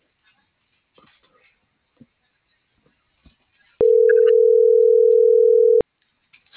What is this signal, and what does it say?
Ringback tone (North America), 440 + 480 Hz.